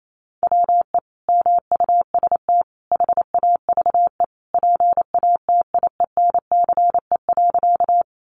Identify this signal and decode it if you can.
Morse code, heard as a keyed tone, about 28 wpm, 700 Hz. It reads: WE MUST HAVE PATIENCE.
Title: Sentences from Top 1300 Words 28wpm